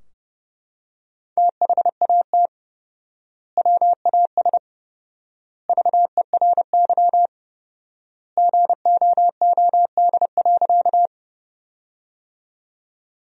Morse code, heard as a keyed tone, about 30 wpm, 700 Hz. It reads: THAT WAS VERY GOOD.
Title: 4x Word-Spacing - Sentences from Top 200 Words - 30wpm